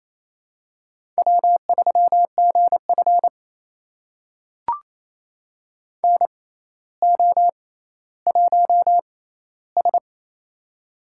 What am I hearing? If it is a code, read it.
Morse code, heard as a keyed tone, about 28 wpm, spaced as for 14 wpm, 700 Hz. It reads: W3GF NO1S
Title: USA Call Signs - 6x Character Spacing - 28wpm